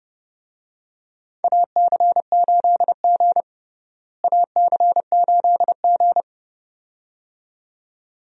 Morse code, heard as a keyed tone, about 30 wpm, 700 Hz. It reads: AC8G AC8G
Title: USA Call Signs - 4 Characters - Repeated 2x - 30wpm